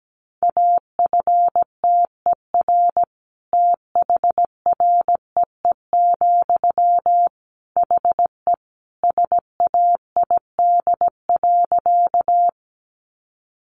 Morse code, heard as a keyed tone, about 17 wpm, 700 Hz. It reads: AFTER THREE, HE SAID.
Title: Sentences from Top 200 Words 17wpm